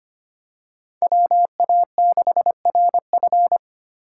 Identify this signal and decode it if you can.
Morse code, heard as a keyed tone, about 25 wpm, 700 Hz. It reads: WA6RF
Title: USA Call Signs - 4x Character Spacing - 25wpm